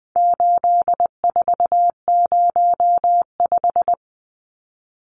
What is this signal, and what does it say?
Morse code, heard as a keyed tone, about 20 wpm, 700 Hz. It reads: 8405
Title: Numbers - 4-Digit 20wpm